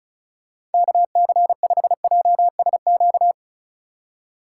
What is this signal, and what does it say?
Morse code, heard as a keyed tone, about 36 wpm, 700 Hz. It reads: KC5JSQ
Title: USA Call Signs - 6 Characters - 3x Character Spacing - 35wpm